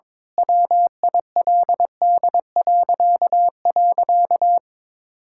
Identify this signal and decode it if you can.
Morse code, heard as a keyed tone, about 21 wpm, 700 Hz. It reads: WILD..
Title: Pangram Challenge v2 - 22wpm